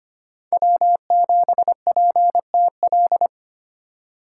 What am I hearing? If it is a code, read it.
Morse code, heard as a keyed tone, about 25 wpm, 700 Hz. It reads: W7PTL
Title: USA Call Signs - 5 Characters - 3x Character Spacing - 25wpm